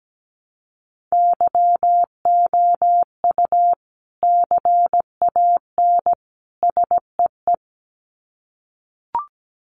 Morse code, heard as a keyed tone, about 17 wpm, 700 Hz. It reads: YOU CAN SEE
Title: Sets of 3 Words 17wpm